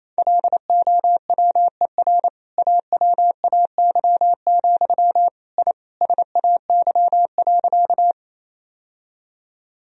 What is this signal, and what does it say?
Morse code, heard as a keyed tone, about 28 wpm, 700 Hz. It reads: LOWER AWAY, I SAY.
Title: Sentences from Top 1000 Words - Review Part 4 - 28wpm